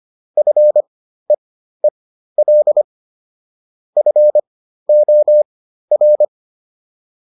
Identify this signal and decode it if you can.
Morse code, heard as a keyed tone, about 25 wpm, spaced as for 13 wpm, 600 Hz. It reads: FEEL FOR